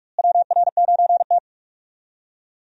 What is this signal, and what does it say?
Morse code, heard as a keyed tone, about 44 wpm, 700 Hz. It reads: WR9T
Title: USA Call Signs - 4 Characters - Repeated 2x - 45wpm